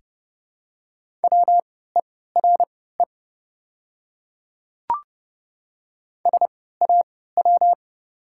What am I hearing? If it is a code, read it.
Morse code, heard as a keyed tone, about 30 wpm, spaced as for 17 wpm, 700 Hz. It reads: WERE SAW
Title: 3x Character Spacing - Top 300 Words - 30wpm